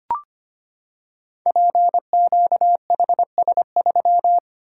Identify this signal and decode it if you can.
Morse code, heard as a keyed tone, about 25 wpm, 700 Hz. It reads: PQHS3
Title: Code Groups - 5 Letter:Numbers - 25wpm